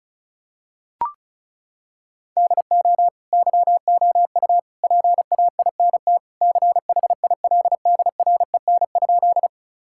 Morse code, heard as a keyed tone, about 36 wpm, 700 Hz. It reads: DO YOU PAINT CHILDREN?
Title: Sentences from Top 1000 Words - Review Part 6 - 35wpm